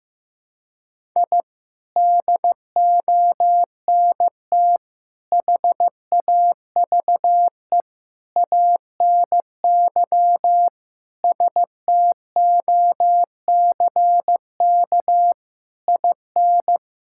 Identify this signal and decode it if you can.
Morse code, heard as a keyed tone, about 15 wpm, 700 Hz. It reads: I DONT HAVE ANY STOCK IN
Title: Sentences from Top 1000 Words - Review Part 5 - 15wpm